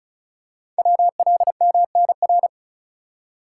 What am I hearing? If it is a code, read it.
Morse code, heard as a keyed tone, about 36 wpm, 690 Hz. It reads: WLMNR
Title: Code Groups - 5 Letters - 35wpm